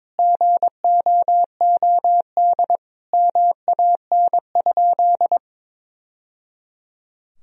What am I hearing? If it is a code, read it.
Morse code, heard as a keyed tone, about 22 wpm, 700 Hz. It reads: GOOD MAN?
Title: Sentences from Top 1000 Words - Review Part 1 - 22wpm